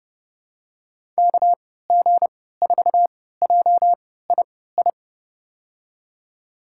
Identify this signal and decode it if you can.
Morse code, heard as a keyed tone, about 30 wpm, spaced as for 17 wpm, 700 Hz. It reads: KG4JII
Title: USA Call Signs - 6 Characters - 3x Character Spacing - 30wpm